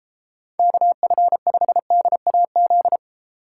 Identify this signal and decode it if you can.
Morse code, heard as a keyed tone, about 32 wpm, 700 Hz. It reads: KF5DAZ